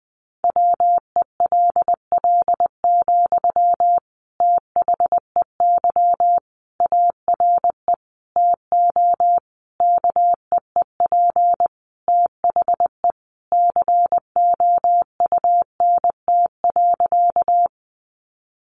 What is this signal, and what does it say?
Morse code, heard as a keyed tone, about 20 wpm, 700 Hz. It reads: WELL, THEY ARE TO KEEP THE COUNT.
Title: Sentences from Top 1500 Words 20wpm